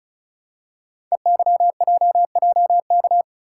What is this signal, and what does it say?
Morse code, heard as a keyed tone, about 33 wpm, 700 Hz. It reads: EYJJK
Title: Code Groups - 5 Letters - 35wpm